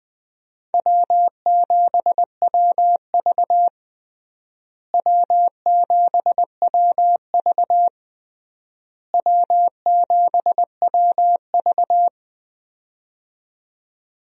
Morse code, heard as a keyed tone, about 21 wpm, 700 Hz. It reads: W7WV W7WV W7WV
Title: USA Call Signs - 4 Characters - Repeated 3x - 20wpm